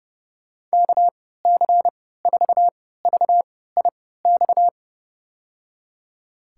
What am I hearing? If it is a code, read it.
Morse code, heard as a keyed tone, about 30 wpm, spaced as for 17 wpm, 700 Hz. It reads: KC4VIX
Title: USA Call Signs - 6 Characters - 3x Character Spacing - 30wpm